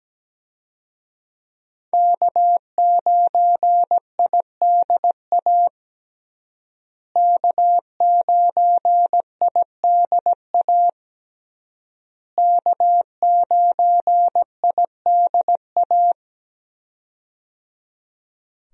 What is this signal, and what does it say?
Morse code, heard as a keyed tone, about 17 wpm, 700 Hz. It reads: K9IDA K9IDA K9IDA
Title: USA Call Signs - 5 Characters - Repeated 3x - 17wpm